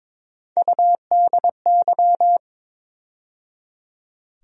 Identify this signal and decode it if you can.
Morse code, heard as a keyed tone, about 22 wpm, 700 Hz. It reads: UDY
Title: USA Call Sign Postfixes - 3 Letters 22wpm